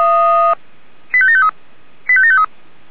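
Five-tone selective call (ZVEI), heard as quick runs of five tones, tones of about 70 ms. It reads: 85753 85753
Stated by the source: ZVEI five-tone selective call received on SDR.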